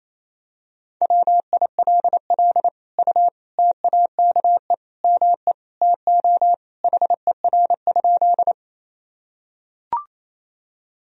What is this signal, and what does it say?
Morse code, heard as a keyed tone, about 28 wpm, 700 Hz. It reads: WILL U TAKE ME TO HER?
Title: Sentences from Top 200 Words and Abbreviations 28wpm